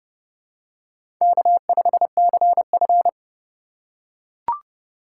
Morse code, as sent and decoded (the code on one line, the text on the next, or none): -.- ..... -.-. ..-.
K5CF